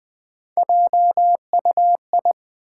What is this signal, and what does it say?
Morse code, heard as a keyed tone, about 20 wpm, 700 Hz. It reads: JUI
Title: Build A Word - 1 Letter At A Time - 5-Letter Words - 20wpm